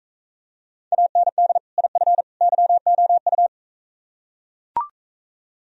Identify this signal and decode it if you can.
Morse code, heard as a keyed tone, about 44 wpm, 700 Hz. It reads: AND IF YOU